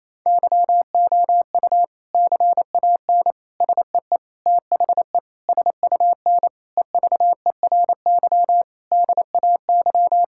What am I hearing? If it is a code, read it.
Morse code, heard as a keyed tone, about 27 wpm, 700 Hz. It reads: YOU CAN SEE THE SUN EVERY DAY
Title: Sentences from Top 1600 Words 28wpm